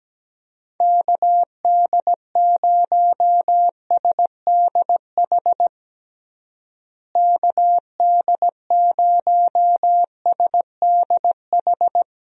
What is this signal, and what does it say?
Morse code, heard as a keyed tone, about 17 wpm, 700 Hz. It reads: KD0SDH KD0SDH
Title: USA Call Signs - 6 Characters - Repeated 3x - 17wpm